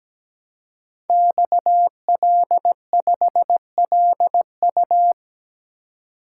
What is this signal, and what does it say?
Morse code, heard as a keyed tone, about 17 wpm, 700 Hz. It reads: XL5LU
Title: Code Groups - 5 Letter:Numbers - 17wpm